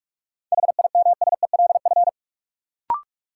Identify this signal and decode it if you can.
Morse code, heard as a keyed tone, about 46 wpm, 700 Hz. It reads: HIMSELF
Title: Sentences from Top 1500 Words - New Word Drill Alt - 45wpm